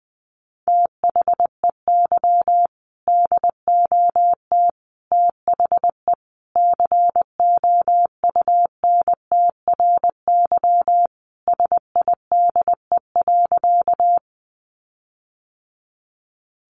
Morse code, heard as a keyed tone, about 19 wpm, 700 Hz. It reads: THEY DOT THE COUNTRYSIDE.
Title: Sentences from Top 1500 Words 20wpm